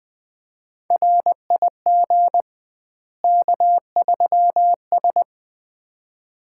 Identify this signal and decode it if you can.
Morse code, heard as a keyed tone, about 20 wpm, 700 Hz. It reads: RIG K3S
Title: Common QSO Phrases - 2x Word-Spacing - 20wpm